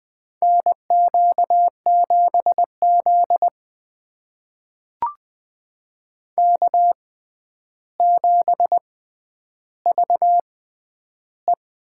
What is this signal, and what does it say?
Morse code, heard as a keyed tone, about 21 wpm, spaced as for 10 wpm, 700 Hz. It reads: NQ7Z K7VE
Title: USA Call Signs - 4 Characters - 6x Character Spacing - 20wpm